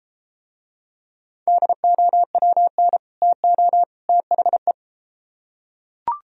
Morse code, heard as a keyed tone, about 33 wpm, 700 Hz. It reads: DOWN TO THE